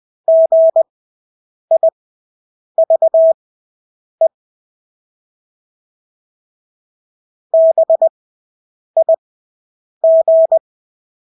Morse code, heard as a keyed tone, about 20 wpm, spaced as for 8 wpm, 650 Hz. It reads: GIVE BIG
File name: Engelske ord 20/8 wpm